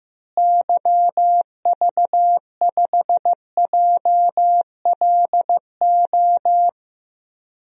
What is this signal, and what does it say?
Morse code, heard as a keyed tone, about 16 wpm, 700 Hz. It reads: YV5JLO
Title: DXCC-100 Call Signs - 6 Characters - Repeated 3x - 15wpm